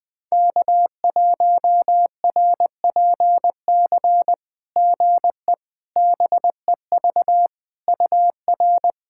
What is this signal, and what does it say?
Morse code, heard as a keyed tone, about 21 wpm, 690 Hz. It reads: K1RPC GE BEV UR
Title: SOTA QSO Practice 20wpm